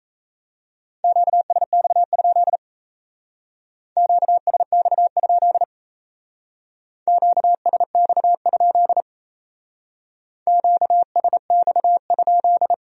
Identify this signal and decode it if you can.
Morse code, heard as a keyed tone, about 35 wpm, 700 Hz. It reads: QSX? QSX? QSX? QSX?